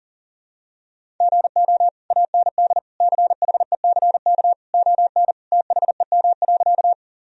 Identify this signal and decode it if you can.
Morse code, heard as a keyed tone, about 40 wpm, 700 Hz. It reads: GO AND CHECK ON THEM.
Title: Sentences from Top 1000 Words - Review Part 3 - 40wpm